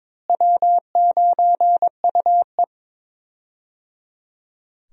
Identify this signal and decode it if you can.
Morse code, heard as a keyed tone, about 21 wpm, 700 Hz. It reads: W9UE
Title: USA Call Signs - 4 Characters - Repeated 3x - 22wpm